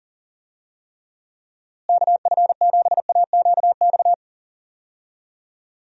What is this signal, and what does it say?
Morse code, heard as a keyed tone, about 40 wpm, 700 Hz. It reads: KF7AQX
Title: USA Call Signs - 6 Characters - Repeated 3x - 40wpm